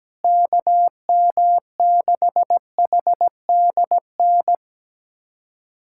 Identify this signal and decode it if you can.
Morse code, heard as a keyed tone, about 17 wpm, 700 Hz. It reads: KM6HDN
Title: USA Callsign DE Callsign - 6 Characters - 17wpm